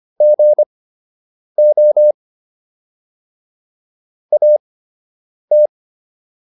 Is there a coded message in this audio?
Yes: Morse code from 0.2 to 5.7 s, heard as a keyed tone, about 25 wpm, spaced as for 8 wpm, 600 Hz. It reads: GO AT